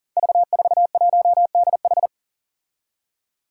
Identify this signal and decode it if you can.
Morse code, heard as a keyed tone, about 38 wpm, 700 Hz. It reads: V41DH
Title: Five Challenging Letter-Number Combinations 40wpm